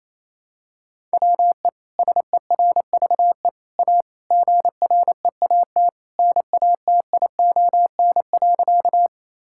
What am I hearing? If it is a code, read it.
Morse code, heard as a keyed tone, about 28 wpm, 700 Hz. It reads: WE SERVE A GREAT NATION.
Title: Sentences from Top 1000 Words - Review Part 6 - 28wpm